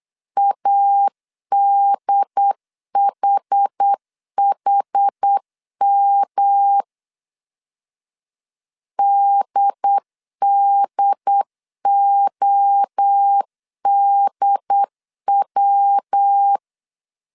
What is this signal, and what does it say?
Morse code, heard as a keyed tone, about 8 wpm, 800 Hz. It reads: ADHHM DDODW